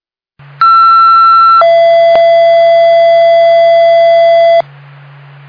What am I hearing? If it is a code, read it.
Two-tone page, heard as a one-second tone, then a three-second tone, about 1340 then 670 Hz, with noise. It is heard from 0.6 to 4.6 s.